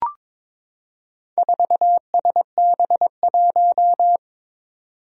Morse code, heard as a keyed tone, about 22 wpm, 700 Hz. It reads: 4SB1
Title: Four Challenging Letter-Number Combinations 22wpm